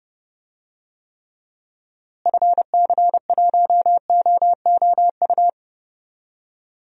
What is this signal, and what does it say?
Morse code, heard as a keyed tone, about 30 wpm, 700 Hz. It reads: FC1OOU
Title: DXCC-100 Call Signs - 6 Characters - Repeated 3x - 30wpm